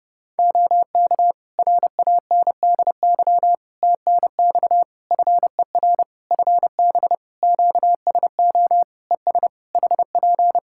Morse code, heard as a keyed tone, about 30 wpm, 700 Hz. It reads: OK RANDY TNX FER FB QSO ES HP
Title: Practice QSO - Exchange 4 - Ending - 30wpm